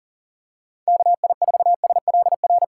Morse code, heard as a keyed tone, about 40 wpm, 700 Hz. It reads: KI4SLR
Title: USA Call Signs - 6x Character Spacing - 40wpm